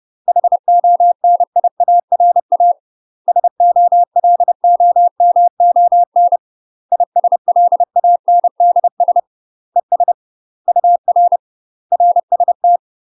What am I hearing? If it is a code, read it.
Morse code, heard as a keyed tone, about 30 wpm, 700 Hz. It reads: HONIARA SOLOMON ISLANDS ES UR RST
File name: UK MASTER QSOs TRAINING / 30WPM / UK MASTER QSOs 0021